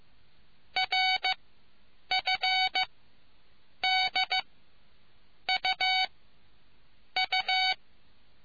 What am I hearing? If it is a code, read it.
Morse code, heard as a keyed tone, about 15 wpm, spaced as for 7 wpm, 750 Hz. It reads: RFDUU